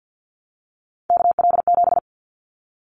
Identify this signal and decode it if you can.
Morse code, heard as a keyed tone, about 50 wpm, 700 Hz. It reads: KL7
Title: Callsign Prefixes for Top 100 DXCC - 50wpm